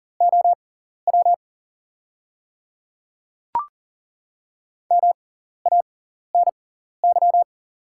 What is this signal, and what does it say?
Morse code, heard as a keyed tone, about 40 wpm, spaced as for 14 wpm, 690 Hz. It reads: OW MANY
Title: 6x Character Spacing - Top 300 Words - 40wpm